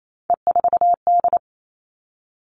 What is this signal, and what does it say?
Morse code, heard as a keyed tone, about 28 wpm, 700 Hz. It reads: E4D
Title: Three Challenging Letter-Number Combinations 28wpm